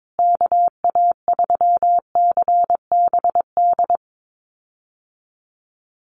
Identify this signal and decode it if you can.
Morse code, heard as a keyed tone, about 22 wpm, 700 Hz. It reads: KA3CBD
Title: USA Call Signs Repeated 3x 22wpm